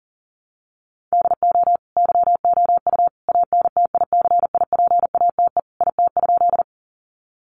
Morse code, heard as a keyed tone, about 40 wpm, 700 Hz. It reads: DO YOU ANTICIPATE IT?
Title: Sentences from Top 1500 Words 40wpm